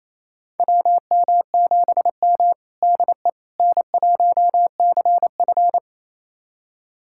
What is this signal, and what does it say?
Morse code, heard as a keyed tone, about 28 wpm, 700 Hz. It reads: WM7M DE N1CF
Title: USA Callsign DE Callsign - 4 Characters - 28wpm